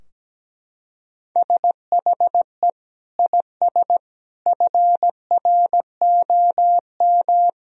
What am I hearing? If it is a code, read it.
Morse code, heard as a keyed tone, about 17 wpm, 700 Hz. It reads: SHE IS FROM
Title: Sentences from Top 1000 Words - Review Part 2 - 17wpm